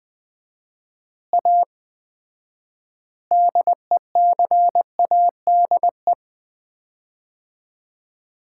Morse code, heard as a keyed tone, about 20 wpm, 700 Hz. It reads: A DECADE
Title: 4x Word-Spacing - Sets of 5 Words - 20wpm